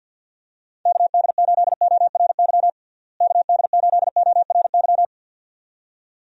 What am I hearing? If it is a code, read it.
Morse code, heard as a keyed tone, about 49 wpm, 700 Hz. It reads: KD8ORY KD8ORY
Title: USA Call Signs - 6 Characters - Repeated 2x - 50wpm